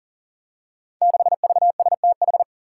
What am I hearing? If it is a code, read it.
Morse code, heard as a keyed tone, about 43 wpm, 700 Hz. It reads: 6VSTH
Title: Five Challenging Letter-Number Combinations 40wpm